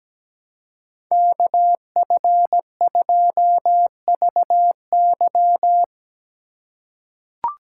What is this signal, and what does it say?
Morse code, heard as a keyed tone, about 17 wpm, 700 Hz. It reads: KF2VY
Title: USA Call Signs - 5 Characters - 4x Character Spacing - 17wpm